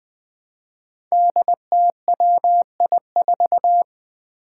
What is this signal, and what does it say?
Morse code, heard as a keyed tone, about 20 wpm, 700 Hz. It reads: DTWI4
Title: Five Challenging Letter-Number Combinations 20wpm